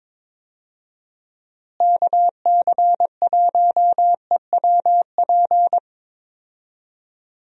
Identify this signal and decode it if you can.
Morse code, heard as a keyed tone, about 22 wpm, 700 Hz. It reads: KC1EWP
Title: USA Call Signs - 6 Characters - Repeated 2x - 22wpm